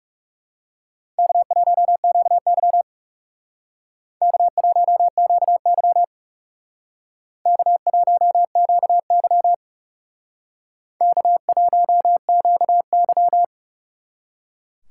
Morse code, heard as a keyed tone, about 39 wpm, 700 Hz. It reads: K1QY K1QY K1QY K1QY